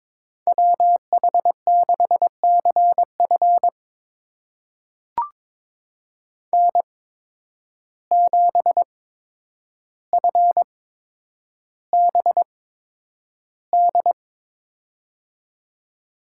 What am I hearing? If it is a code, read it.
Morse code, heard as a keyed tone, about 22 wpm, spaced as for 9 wpm, 700 Hz. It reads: WH6CF N7FBD
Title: USA Call Signs - 8x Character Spacing - 22wpm